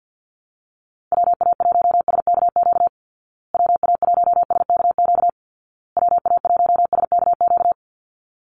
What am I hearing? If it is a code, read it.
Morse code, heard as a keyed tone, about 50 wpm, 700 Hz. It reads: WA1SKQ WA1SKQ WA1SKQ
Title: Call Signs Repeated 2x - 50wpm